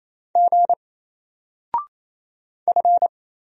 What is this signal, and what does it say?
Morse code, heard as a keyed tone, about 27 wpm, 700 Hz. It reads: G F
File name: Single-Letter-Number-28wpm